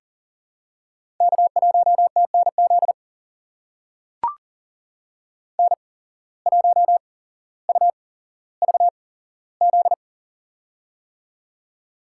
Morse code, heard as a keyed tone, about 42 wpm, spaced as for 17 wpm, 700 Hz. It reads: K1TNZ N1UVZ
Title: USA Call Signs - 5 Characters - 8x Character Spacing - 40wpm